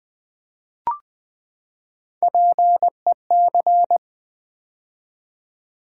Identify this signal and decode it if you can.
Morse code, heard as a keyed tone, about 20 wpm, 700 Hz. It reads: PEC
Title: Build A Word - 3 Letters At A Time - 8-Letter Words - 20wpm